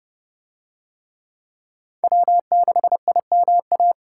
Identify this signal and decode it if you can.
Morse code, heard as a keyed tone, about 26 wpm, 700 Hz. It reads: W6IMA